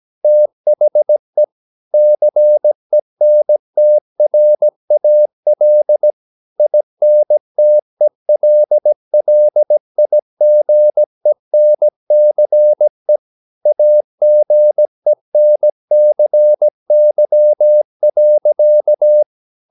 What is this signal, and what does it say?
Morse code, heard as a keyed tone, about 17 wpm, 600 Hz. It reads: THE CENTRAL INTELLIGENCE AGENCY.
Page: Romanen Never af Ken Follet på engelsk.